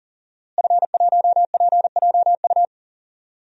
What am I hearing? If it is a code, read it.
Morse code, heard as a keyed tone, about 40 wpm, 700 Hz. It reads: F1PJU
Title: DXCC-100 Call Signs - 5 Characters - Repeated 2x - 40wpm